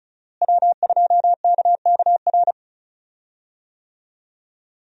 Morse code, heard as a keyed tone, about 36 wpm, 700 Hz. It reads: W2KKR
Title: USA Call Signs - 5 Characters - Repeated 2x - 35wpm